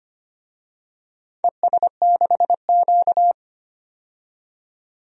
Morse code, heard as a keyed tone, about 25 wpm, 700 Hz. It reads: ES6Q
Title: DXCC-100 Call Signs - 4 Characters - 25wpm